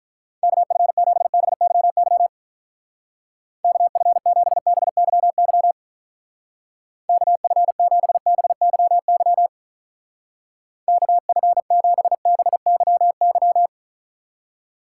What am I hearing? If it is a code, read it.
Morse code, heard as a keyed tone, about 43 wpm, 700 Hz. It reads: KF7BYY KF7BYY KF7BYY KF7BYY
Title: Speed Racing - USA 2x3 Call Signs 35wpm